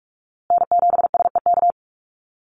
Morse code, heard as a keyed tone, about 44 wpm, 700 Hz. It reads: N7SEK
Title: USA Call Signs Repeated 2x 45wpm